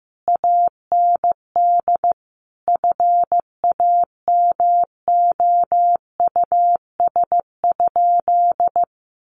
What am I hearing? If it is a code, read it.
Morse code, heard as a keyed tone, about 15 wpm, 700 Hz. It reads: AND FAMOUS?
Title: Sentences from Top 1500 Words 15wpm